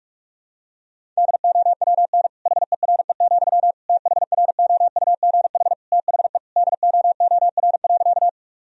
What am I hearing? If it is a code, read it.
Morse code, heard as a keyed tone, about 42 wpm, 700 Hz. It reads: DOWN HERE, THROUGH THE DOOR.
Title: Sentences from Top 1000 Words - Review Part 1 - 45wpm